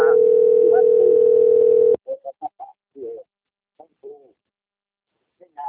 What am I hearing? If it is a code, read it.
Ringback tone (North America), 440 + 480 Hz, with speech.
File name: out- - - - - .18173